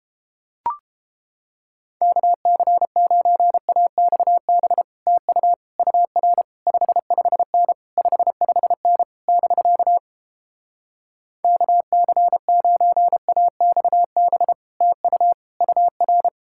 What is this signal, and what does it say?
Morse code, heard as a keyed tone, about 29 wpm, 700 Hz. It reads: KC9AXB TU UR 55N 55N <BK> KC9AXB TU UR